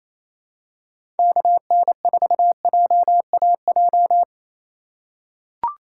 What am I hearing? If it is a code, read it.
Morse code, heard as a keyed tone, about 28 wpm, 700 Hz. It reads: KN4JAJ
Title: USA Call Signs - 6 Characters - 4x Character Spacing - 28wpm